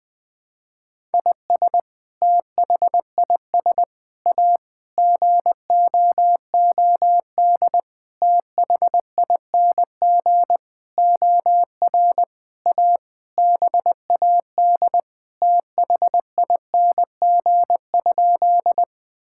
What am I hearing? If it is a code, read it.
Morse code, heard as a keyed tone, about 20 wpm, 700 Hz. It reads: IS THIS A GOOD THING OR A BAD THING?
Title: Sentences with 7-Word Phrases - 20wpm